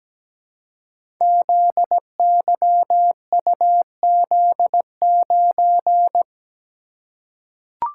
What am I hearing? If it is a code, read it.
Morse code, heard as a keyed tone, about 17 wpm, 700 Hz. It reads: ZYUZ9
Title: Code Groups - 5 Letter:Numbers - 17wpm